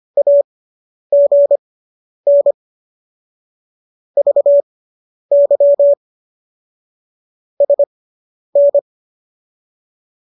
Morse code, heard as a keyed tone, about 25 wpm, spaced as for 10 wpm, 570 Hz. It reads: AGN VY SN